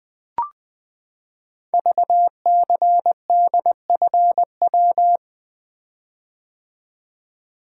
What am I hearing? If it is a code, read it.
Morse code, heard as a keyed tone, about 20 wpm, 700 Hz. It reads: VCDFW
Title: Code Groups - 5 Letters - 20wpm